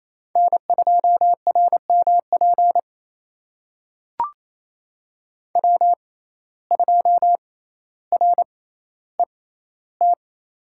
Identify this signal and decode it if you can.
Morse code, heard as a keyed tone, about 28 wpm, spaced as for 14 wpm, 700 Hz. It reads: N2RMP W2RET
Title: USA Call Signs - 5 Characters - 6x Character Spacing - 28wpm